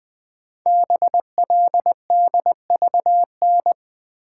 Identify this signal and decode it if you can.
Morse code, heard as a keyed tone, about 20 wpm, 700 Hz. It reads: BLDVN
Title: Code Groups - 5 Letter:Numbers - 20wpm